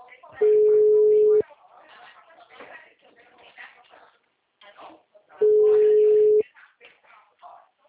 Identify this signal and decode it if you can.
Ringback tone (Europe, 425 Hz), with speech.
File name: out- -2017- - - .69499